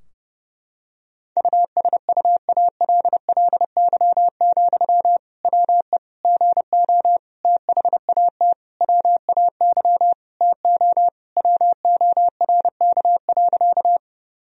Morse code, heard as a keyed tone, about 32 wpm, 700 Hz. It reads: USUALLY, WE GO THAT WAY TO WORK.
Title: Sentences from Top 1000 Words - Review Part 3 - 30wpm